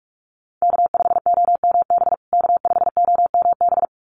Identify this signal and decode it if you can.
Morse code, heard as a keyed tone, about 44 wpm, 700 Hz. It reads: K5OMB K5OMB
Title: USA Call Signs Repeated 2x 45wpm